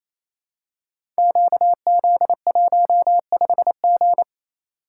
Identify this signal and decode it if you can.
Morse code, heard as a keyed tone, about 27 wpm, 700 Hz. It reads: QZ15G
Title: Code Groups - 5 Letter:Numbers - 28wpm